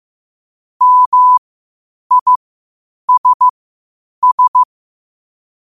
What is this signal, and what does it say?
Morse code, heard as a keyed tone, about 15 wpm, spaced as for 8 wpm, 1000 Hz. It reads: MISS
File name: 8wpm